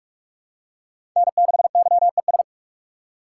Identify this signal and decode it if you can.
Morse code, heard as a keyed tone, about 43 wpm, 700 Hz. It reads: N6YES
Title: USA Call Signs - 3x Character Spacing - 45wpm